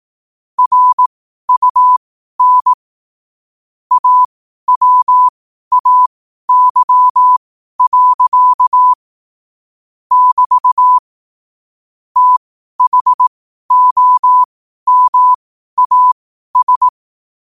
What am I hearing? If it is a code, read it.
Morse code, heard as a keyed tone, about 18 wpm, spaced as for 12 wpm, 1000 Hz. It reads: RUN AWAY. <BT> THOMAS
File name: qotdfiles / 12wpm